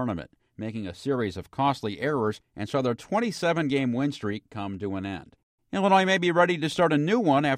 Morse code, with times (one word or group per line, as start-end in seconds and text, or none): none